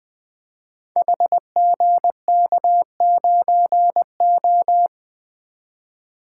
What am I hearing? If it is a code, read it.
Morse code, heard as a keyed tone, about 20 wpm, 700 Hz. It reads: HGK9O
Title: Code Groups - 5 Letter:Numbers - 20wpm